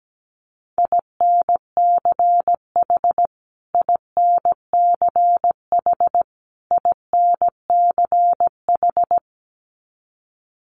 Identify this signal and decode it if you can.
Morse code, heard as a keyed tone, about 17 wpm, 700 Hz. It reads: INCH INCH INCH
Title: Top 500 Words Repeated 17wpm